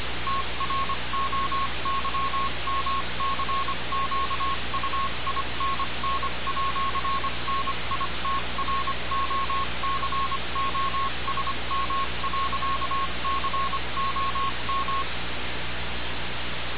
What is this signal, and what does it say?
Morse code, heard as a keyed tone, about 25 wpm, 1090 Hz, with noise. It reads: TROYMCQUINN@NITROCOSM.COM
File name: radiocap / REC z